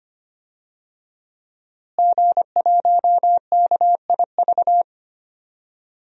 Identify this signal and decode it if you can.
Morse code, heard as a keyed tone, about 25 wpm, 700 Hz. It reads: G1KIV
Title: DXCC-100 Call Signs - 5 Characters - Repeated 2x - 25wpm